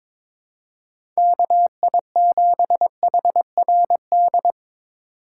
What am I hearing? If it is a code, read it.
Morse code, heard as a keyed tone, about 22 wpm, 700 Hz. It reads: KI7HRD
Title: USA Call Signs - 6 Characters - 2x Character Spacing - 22wpm